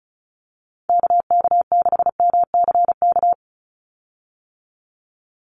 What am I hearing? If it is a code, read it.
Morse code, heard as a keyed tone, about 34 wpm, 700 Hz. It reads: KK6MCK
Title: USA Call Signs 35wpm